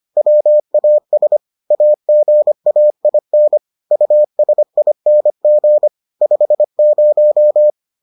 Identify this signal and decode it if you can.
Morse code, heard as a keyed tone, about 25 wpm, 600 Hz. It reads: WAS AGAIN USING 50